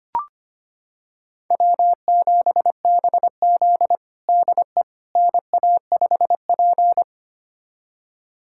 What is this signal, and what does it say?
Morse code, heard as a keyed tone, about 25 wpm, 700 Hz. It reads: W7BZ DE NA5P
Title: USA Callsign DE Callsign - 4 Characters - 25wpm